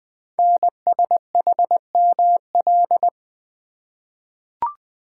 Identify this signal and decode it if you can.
Morse code, heard as a keyed tone, about 20 wpm, 700 Hz. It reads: NSHML
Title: Code Groups - 5 Letters - 20wpm